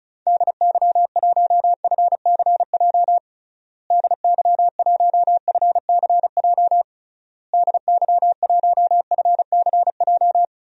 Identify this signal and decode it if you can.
Morse code, heard as a keyed tone, about 36 wpm, 700 Hz. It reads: DY1FCJ DY1FCJ DY1FCJ
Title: DXCC-100 Call Signs - 6 Characters - Repeated 3x - 35wpm